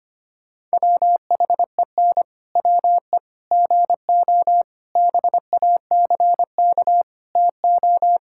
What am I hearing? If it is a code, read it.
Morse code, heard as a keyed tone, about 25 wpm, 700 Hz. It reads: WHEN WE GO BACK TO
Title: Pangram Challenge v2 - 25wpm